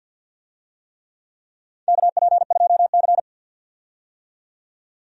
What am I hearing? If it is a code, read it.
Morse code, heard as a keyed tone, about 52 wpm, 700 Hz. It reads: KP2C
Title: DXCC-100 Call Signs - 4 Characters - 50wpm